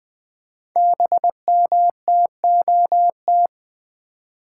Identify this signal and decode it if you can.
Morse code, heard as a keyed tone, about 20 wpm, 700 Hz. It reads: BMTOT
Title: Code Groups - 5 Letters - 20wpm